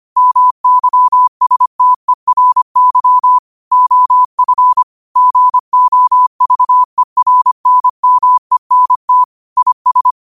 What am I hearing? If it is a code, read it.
Morse code, heard as a keyed tone, about 25 wpm, 1000 Hz. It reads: MYSTERY OF GOVERNMENT IS